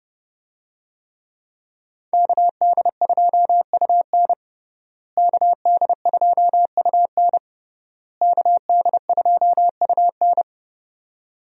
Morse code, heard as a keyed tone, about 30 wpm, 700 Hz. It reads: KD2UN KD2UN KD2UN
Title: USA Call Signs - 5 Characters - Repeated 3x - 30wpm